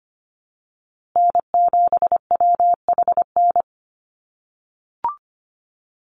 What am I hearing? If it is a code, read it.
Morse code, heard as a keyed tone, about 25 wpm, 700 Hz. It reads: N7WHN
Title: USA Call Signs Repeated 2x 25wpm